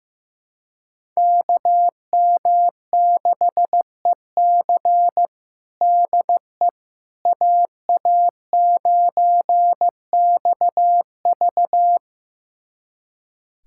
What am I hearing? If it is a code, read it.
Morse code, heard as a keyed tone, about 16 wpm, 700 Hz. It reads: KM6EC DE AA9XV
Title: USA Callsign DE Callsign - 5 Characters - 15wpm